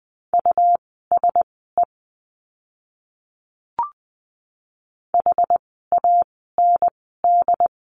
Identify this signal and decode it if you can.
Morse code, heard as a keyed tone, about 20 wpm, spaced as for 15 wpm, 700 Hz. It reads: USE HAND
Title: 237 Lesson 38f - Letter q - Words 20wpm